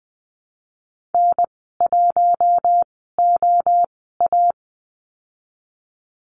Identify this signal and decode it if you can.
Morse code, heard as a keyed tone, about 20 wpm, spaced as for 14 wpm, 700 Hz. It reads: N1OA
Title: 042 Lesson 8g - Number 1 - Call Signs 20wpm